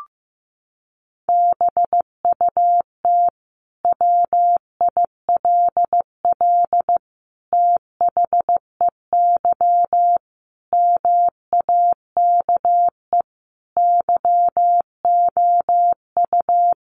Morse code, heard as a keyed tone, about 15 wpm, 700 Hz. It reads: BUT WILL THEY MAKE YOU
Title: Sentences from Top 1000 Words 15wpm